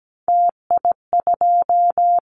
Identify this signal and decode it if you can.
Morse code, heard as a keyed tone, about 17 wpm, 700 Hz. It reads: TI2
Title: Callsign Prefixes for Top 100 DXCC - 17wpm